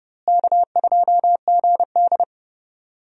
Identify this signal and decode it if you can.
Morse code, heard as a keyed tone, about 30 wpm, 700 Hz. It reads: K2GD
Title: USA Call Signs - 4 Characters - Repeated 2x - 30wpm